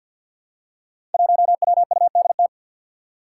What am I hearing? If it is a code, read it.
Morse code, heard as a keyed tone, about 48 wpm, 700 Hz. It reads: 1WUDT
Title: Five Challenging Letter-Number Combinations 50wpm